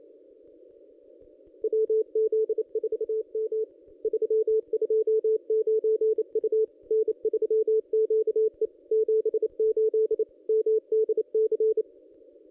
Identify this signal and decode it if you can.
Morse code, heard as a keyed tone, about 28 wpm, 440 Hz. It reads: WZ4M 329 U N3QE 78 MDC
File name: contests / ss-cw-k5tr-op-n5rz -1400